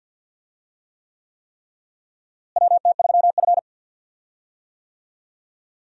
Morse code, heard as a keyed tone, about 52 wpm, 700 Hz. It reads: WT3F
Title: USA Call Signs - 4 Characters - Repeated 3x - 50wpm